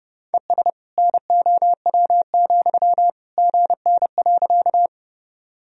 Morse code, heard as a keyed tone, about 32 wpm, 700 Hz. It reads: ES NOW, GN.
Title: Sentences from Top 300 Words and Abbreviations 30wpm